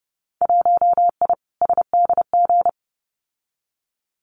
Morse code, heard as a keyed tone, about 30 wpm, 700 Hz. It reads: 1I SDG
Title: Winter Field Day Exchanges 30wpm